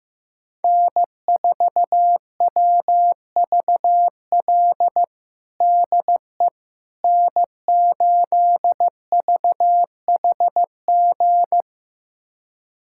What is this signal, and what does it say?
Morse code, heard as a keyed tone, about 15 wpm, 700 Hz. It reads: N4WVL DE N8VHG
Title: USA Callsign DE Callsign - 5 Characters - 15wpm